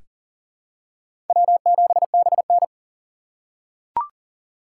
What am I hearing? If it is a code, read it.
Morse code, heard as a keyed tone, about 40 wpm, 700 Hz. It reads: W7BN
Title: USA Call Signs - 3x Character Spacing - 40wpm